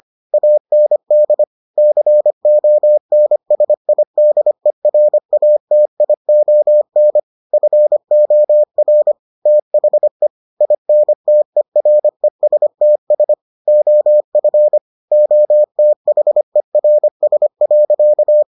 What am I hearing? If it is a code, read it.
Morse code, heard as a keyed tone, about 25 wpm, 600 Hz. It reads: AND CONSIDERATION FOR THE INTERESTS OF OTHERS.